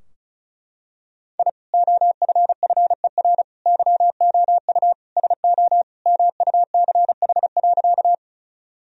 Morse code, heard as a keyed tone, about 34 wpm, 700 Hz. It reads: I OFFER YOU SO MUCH.
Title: Sentences from Top 1000 Words - Review Part 4 - 35wpm